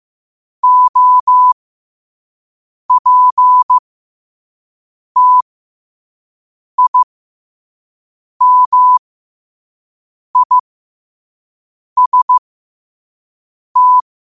Morse code, heard as a keyed tone, about 15 wpm, spaced as for 5 wpm, 1000 Hz. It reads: OPTIMIST
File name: qotdfiles / 5wpm